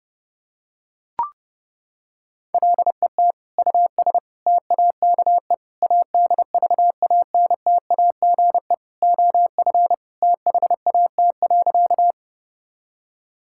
Morse code, heard as a keyed tone, about 32 wpm, 700 Hz. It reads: LET US TAKE ADVANTAGE OF THAT.
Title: Sentences from Top 1000 Words - Review Part 6 - 30wpm